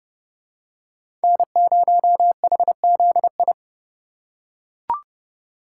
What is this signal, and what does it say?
Morse code, heard as a keyed tone, about 29 wpm, 700 Hz. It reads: N0HZI